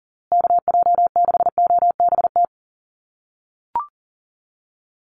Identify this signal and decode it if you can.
Morse code, heard as a keyed tone, about 40 wpm, 700 Hz. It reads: KJ6OBT
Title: USA Call Signs Repeated 3x 40wpm